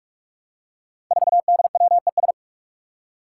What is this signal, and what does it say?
Morse code, heard as a keyed tone, about 44 wpm, 700 Hz. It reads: 4DWES
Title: Five Challenging Letter-Number Combinations 45wpm